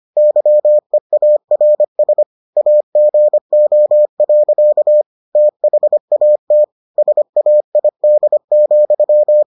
Morse code, heard as a keyed tone, about 25 wpm, 600 Hz. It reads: YEARS AGO. THAT SAID,